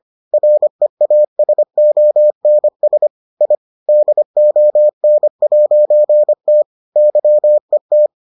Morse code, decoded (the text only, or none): REASONS I DON'T YET